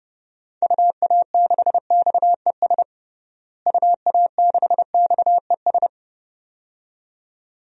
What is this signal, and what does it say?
Morse code, heard as a keyed tone, about 30 wpm, 700 Hz. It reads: UA6XES UA6XES
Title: DXCC-100 Call Signs - 6 Characters - Repeated 2x - 30wpm